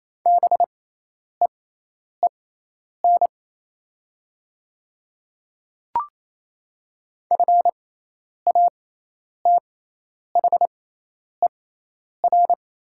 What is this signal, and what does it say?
Morse code, heard as a keyed tone, about 28 wpm, spaced as for 10 wpm, 700 Hz. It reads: BEEN FATHER
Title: 6x Character Spacing - Top 200 Words - 28wpm